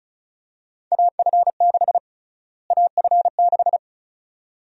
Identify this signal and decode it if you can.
Morse code, heard as a keyed tone, about 36 wpm, 700 Hz. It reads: AF6 AF6
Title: USA Call Sign Prefixes - Repeated 2x - 35wpm